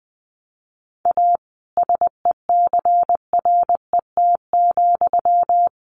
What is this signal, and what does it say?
Morse code, heard as a keyed tone, about 19 wpm, 700 Hz. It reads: A SECRET,
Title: Sentences from Top 1200 Words 20wpm